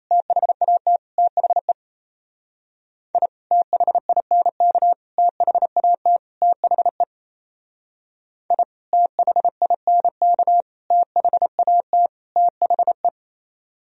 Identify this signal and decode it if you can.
Morse code, heard as a keyed tone, about 34 wpm, 700 Hz. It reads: THAT THE I THINK THAT THE I THINK THAT THE
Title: Speed-Racing - Sets of 4 Words 28wpm